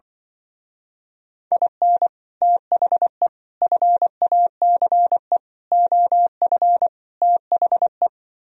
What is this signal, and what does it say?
Morse code, heard as a keyed tone, about 25 wpm, 700 Hz. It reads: IN THE FACE OF THE
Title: Speed-Racing - Sets of 5 Words 20wpm